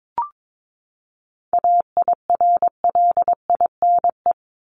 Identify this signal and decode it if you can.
Morse code, heard as a keyed tone, about 22 wpm, 700 Hz. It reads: AIRLINE
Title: Top 1501-1750 Words 22wpm